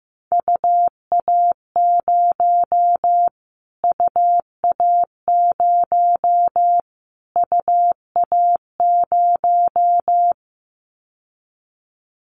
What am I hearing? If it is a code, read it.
Morse code, heard as a keyed tone, about 15 wpm, 700 Hz. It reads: UA0 UA0 UA0
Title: Callsign Prefixes for Top 25 DXCC - Repeated 3x - 15wpm